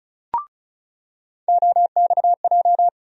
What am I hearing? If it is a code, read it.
Morse code, heard as a keyed tone, about 36 wpm, 700 Hz. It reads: OXJ
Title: USA Call Sign Postfixes - 3 Letters 35wpm